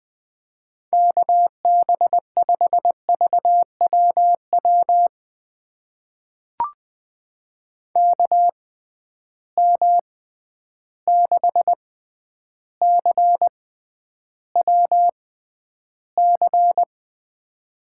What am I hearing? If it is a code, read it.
Morse code, heard as a keyed tone, about 20 wpm, spaced as for 10 wpm, 700 Hz. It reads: KB5VWW KM6CWC
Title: USA Call Signs - 6 Characters - 6x Character Spacing - 20wpm